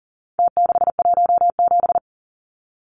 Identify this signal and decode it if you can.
Morse code, heard as a keyed tone, about 39 wpm, 700 Hz. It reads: T617
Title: Numbers - Contest Style - 4-Digit 40wpm